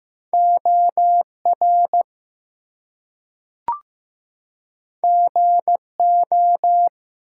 Morse code, heard as a keyed tone, about 15 wpm, 700 Hz. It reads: OR GO
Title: Two Letter Words - 15wpm